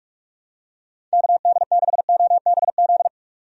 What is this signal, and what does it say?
Morse code, heard as a keyed tone, about 46 wpm, 700 Hz. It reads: KD6OBZ
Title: USA Call Signs - 6 Characters - 45wpm